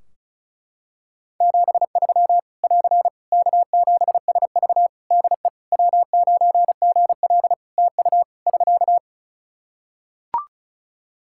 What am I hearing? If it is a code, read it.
Morse code, heard as a keyed tone, about 36 wpm, 700 Hz. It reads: 73 <AR> K7SV DE W9GL TU <SK>
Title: Practice QSO - Exchange 4 - Ending - 35wpm